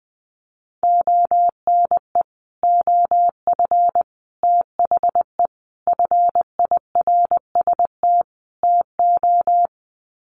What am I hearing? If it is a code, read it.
Morse code, heard as a keyed tone, about 20 wpm, 700 Hz. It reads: ONE OF THE FIRST TO
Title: Sets of 5 Words Encore 20wpm